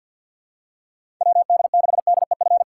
Morse code, heard as a keyed tone, about 47 wpm, 700 Hz. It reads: WD6DEF
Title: USA Call Signs - 6 Characters - 6x Character Spacing - 50wpm